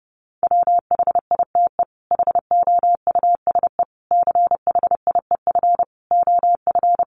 Morse code, heard as a keyed tone, about 28 wpm, 700 Hz. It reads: WHITE HOUSE CHIEF OF
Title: Sets of 5 Words Encore 30wpm